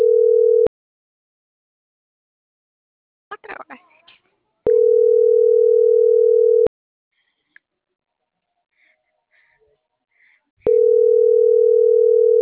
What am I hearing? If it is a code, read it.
Ringback tone (North America), 440 + 480 Hz.